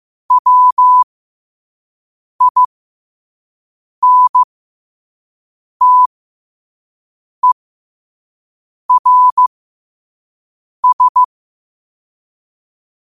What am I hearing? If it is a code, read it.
Morse code, heard as a keyed tone, about 15 wpm, spaced as for 5 wpm, 1000 Hz. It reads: WINTERS